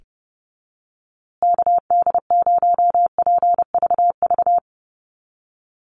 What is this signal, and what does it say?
Morse code, heard as a keyed tone, about 30 wpm, 700 Hz. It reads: KD0PVV
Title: USA Call Signs Repeated 3x 30wpm